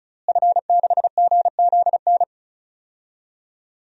Morse code, heard as a keyed tone, about 36 wpm, 700 Hz. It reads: F6GZN
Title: DXCC-100 Call Signs - 5 Characters - Repeated 3x - 35wpm